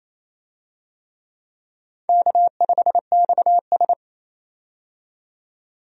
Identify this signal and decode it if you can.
Morse code, heard as a keyed tone, about 28 wpm, 700 Hz. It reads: K5XS
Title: USA Call Signs - 4 Characters - 28wpm